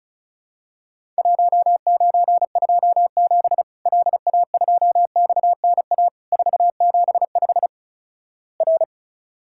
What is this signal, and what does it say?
Morse code, heard as a keyed tone, about 34 wpm, 700 Hz. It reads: 1927 LA2XNA 475 R
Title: Worked All Europe DX Contest - QTC Exchange - 35wpm